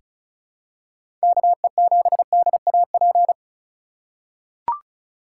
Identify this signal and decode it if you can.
Morse code, heard as a keyed tone, about 36 wpm, 700 Hz. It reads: KE7DAP
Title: USA Call Signs - 6 Characters - 3x Character Spacing - 35wpm